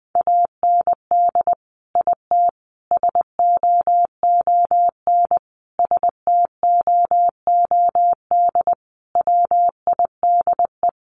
Morse code, heard as a keyed tone, about 20 wpm, 700 Hz. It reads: AND IT SOON STOOD WIDE
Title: Sentences from Top 900 Words 20wpm